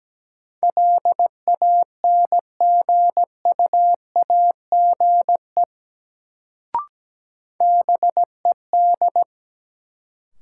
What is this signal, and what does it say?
Morse code, heard as a keyed tone, about 17 wpm, 700 Hz. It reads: LANGUAGE BED
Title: Sentences from Top 500 Words - New Word Drill - 17wpm